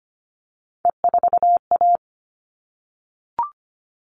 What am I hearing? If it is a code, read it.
Morse code, heard as a keyed tone, about 25 wpm, 700 Hz. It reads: E4A
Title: All Cut Numbers - 3-Digit 25wpm